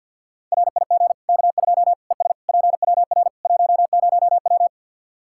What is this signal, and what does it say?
Morse code, heard as a keyed tone, about 52 wpm, 700 Hz. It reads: RIG K2 ES PWR 10W
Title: Practice QSO - Exchange 2 - Info - 50wpm